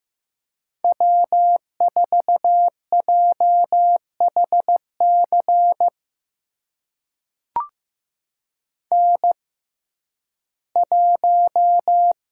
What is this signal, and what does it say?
Morse code, heard as a keyed tone, about 15 wpm, spaced as for 11 wpm, 700 Hz. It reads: W4JHC N1
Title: USA Call Signs - 5 Characters - 6x Character Spacing - 15wpm